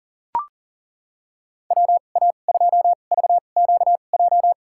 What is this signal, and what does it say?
Morse code, heard as a keyed tone, about 37 wpm, spaced as for 28 wpm, 700 Hz. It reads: WA2VQJ
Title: USA Call Signs - 6 Characters - 2x Character Spacing - 40wpm